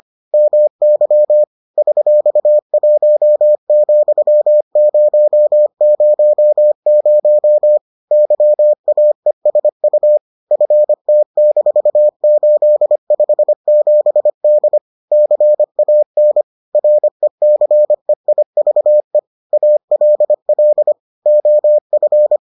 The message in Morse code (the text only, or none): MY $1,000 YAESU FT-857D CAN RECEIVE ALL OF